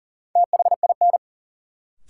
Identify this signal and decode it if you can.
Morse code, heard as a keyed tone, about 40 wpm, 700 Hz. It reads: THIN
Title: Sentences from Top 1200 Words - New Word Drill Alt - 40wpm